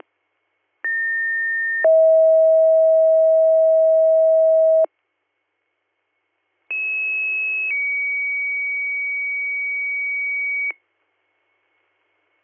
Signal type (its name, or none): two-tone page